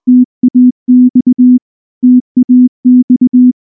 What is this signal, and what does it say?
Morse code, heard as a keyed tone, about 19 wpm, 250 Hz. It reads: TAX TAX
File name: morse / morse-training-20WPM-2